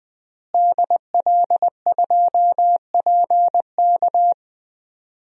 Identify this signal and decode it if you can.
Morse code, heard as a keyed tone, about 20 wpm, 700 Hz. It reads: DL2PK
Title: DXCC-100 Call Signs - 5 Characters - Repeated 3x - 20wpm